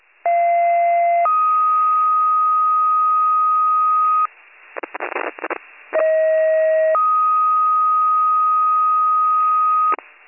Two-tone page, heard as a one-second tone, then a three-second tone, about 690 then 1250 Hz, about 630 then 1250 Hz. It is heard from 0.2 to 9.9 s.